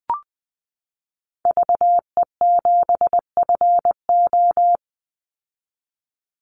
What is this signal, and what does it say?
Morse code, heard as a keyed tone, about 20 wpm, 700 Hz. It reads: VE7FO
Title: Call Signs Repeated 3x - 20wpm